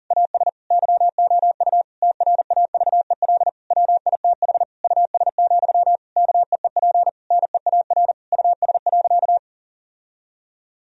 Morse code, heard as a keyed tone, about 37 wpm, 700 Hz. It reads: AS YOU TRAVEL WITH US, KEEP NEAR US.
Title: Sentences from Top 1000 Words - Review Part 2 - 40wpm